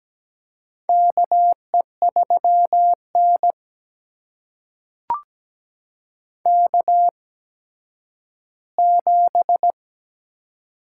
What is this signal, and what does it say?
Morse code, heard as a keyed tone, about 17 wpm, spaced as for 10 wpm, 700 Hz. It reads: KE3N K7
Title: USA Call Signs - 4 Characters - 8x Character Spacing - 17wpm